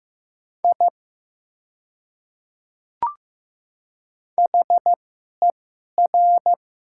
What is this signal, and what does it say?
Morse code, heard as a keyed tone, about 15 wpm, spaced as for 11 wpm, 700 Hz. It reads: I HER
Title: 2x Character Spacing - Top 200 Words - 15wpm